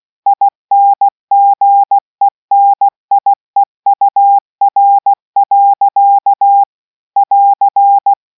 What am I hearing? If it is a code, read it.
Morse code, heard as a keyed tone, about 16 wpm, 800 Hz. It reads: INGENIEUR. <AR>